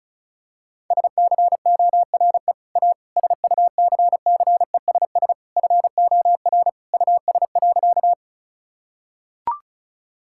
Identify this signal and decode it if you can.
Morse code, heard as a keyed tone, about 34 wpm, 700 Hz. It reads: SCORE A SUCCESS FOR US.
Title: Sentences from Top 1600 Words 35wpm